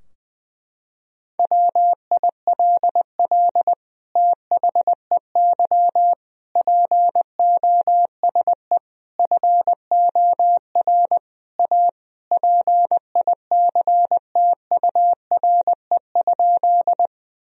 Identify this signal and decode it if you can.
Morse code, heard as a keyed tone, about 20 wpm, 700 Hz. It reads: WILL THEY POSE FOR A PICTURE?
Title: Sentences from Top 1000 Words - Review Part 2 - 20wpm